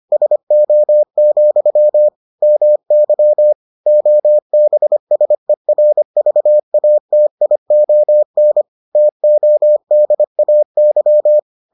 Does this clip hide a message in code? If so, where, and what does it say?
Yes: Morse code from 0.1 to 11.4 s, heard as a keyed tone, about 25 wpm, 600 Hz. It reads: SO, MY OBSERVATION TODAY